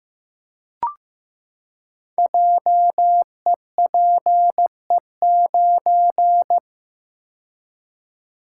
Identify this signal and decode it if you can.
Morse code, heard as a keyed tone, about 15 wpm, 700 Hz. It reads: JEPE9
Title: Code Groups - 5 Letter:Numbers - 15wpm